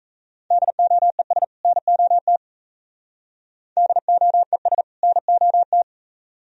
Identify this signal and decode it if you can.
Morse code, heard as a keyed tone, about 32 wpm, 700 Hz. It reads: DOES NOT DOES NOT
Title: Speed-Racing - Sets of 2 Words 28wpm